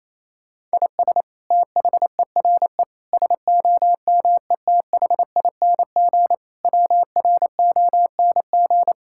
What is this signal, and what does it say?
Morse code, heard as a keyed tone, about 28 wpm, 700 Hz. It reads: IS THERE SOMETHING WRONG
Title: Sentences from Top 1000 Words - Review Part 6 - 28wpm